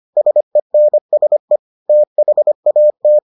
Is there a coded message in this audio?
Yes: Morse code from 0.2 to 3.2 s, heard as a keyed tone, about 25 wpm, 600 Hz. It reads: SENSE THAT